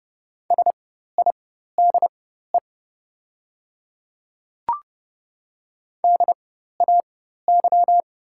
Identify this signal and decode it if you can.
Morse code, heard as a keyed tone, about 30 wpm, spaced as for 14 wpm, 700 Hz. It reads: SIDE DAY
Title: 4x Character Spacing - Top 300 Words - 30wpm